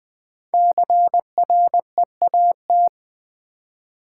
Morse code, heard as a keyed tone, about 20 wpm, 700 Hz. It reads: CREAT
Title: Build A Word - 5 Letters At A Time - 10-Letter Words - 20wpm